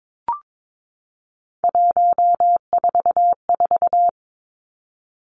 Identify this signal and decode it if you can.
Morse code, heard as a keyed tone, about 22 wpm, 700 Hz. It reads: 144
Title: Three Challenging Letter-Number Combinations 22wpm